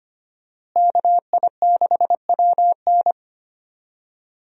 Morse code, heard as a keyed tone, about 25 wpm, 700 Hz. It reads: KI6WN
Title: USA Call Signs - 5 Characters - Repeated 2x - 25wpm